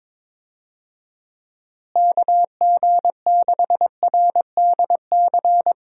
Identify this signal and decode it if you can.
Morse code, heard as a keyed tone, about 22 wpm, 700 Hz. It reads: KG6RDC
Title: USA Call Signs - 6 Characters - Repeated 2x - 22wpm